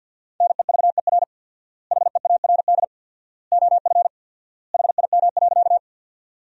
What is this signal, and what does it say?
Morse code, heard as a keyed tone, about 50 wpm, 700 Hz. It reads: NEVER HEARD OF HIM.
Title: 4x Word-Spacing - Sentences from Top 400 Words - 50wpm